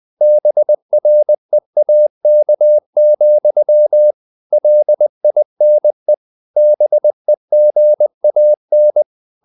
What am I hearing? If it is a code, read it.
Morse code, heard as a keyed tone, about 20 wpm, 600 Hz. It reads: BREAK, LINE BEGAN
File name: the old man and the sea part 2 20wpm